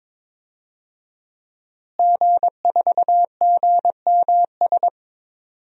Morse code, heard as a keyed tone, about 22 wpm, 700 Hz. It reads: G4GMS
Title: DXCC-100 Call Signs - 5 Characters - Repeated 3x - 22wpm